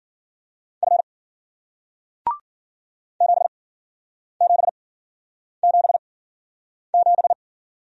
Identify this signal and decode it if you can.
Morse code, heard as a keyed tone, about 54 wpm, spaced as for 9 wpm, 700 Hz. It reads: F 7777